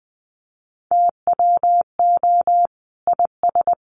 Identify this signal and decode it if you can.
Morse code, heard as a keyed tone, about 19 wpm, 700 Hz. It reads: TWO IS
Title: Sentences from Top 600 Words 20wpm